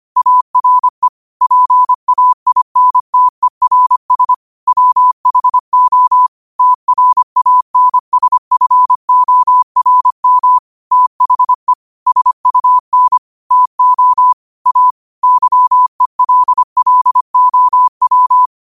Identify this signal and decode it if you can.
Morse code, heard as a keyed tone, about 25 wpm, 1000 Hz. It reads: ARE PAINTERS WHO TRANSFORM THE SUN TO A YELLOW
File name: qotdfiles / 25wpm